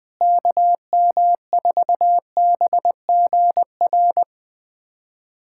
Morse code, heard as a keyed tone, about 20 wpm, 700 Hz. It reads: KM4BGR
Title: USA Callsign DE Callsign 20wpm